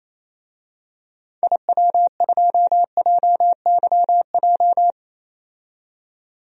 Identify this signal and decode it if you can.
Morse code, heard as a keyed tone, about 28 wpm, 700 Hz. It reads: IW2JYJ
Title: DXCC-100 Call Signs - 6 Characters - 28wpm